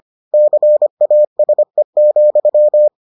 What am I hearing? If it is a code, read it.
Morse code, heard as a keyed tone, about 25 wpm, 600 Hz. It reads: CASE,